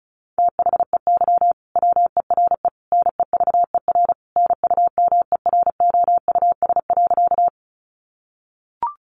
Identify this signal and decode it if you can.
Morse code, heard as a keyed tone, about 34 wpm, 700 Hz. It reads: THEY WERE NEVER NUMEROUS.
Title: Sentences from Top 1400 Words 35wpm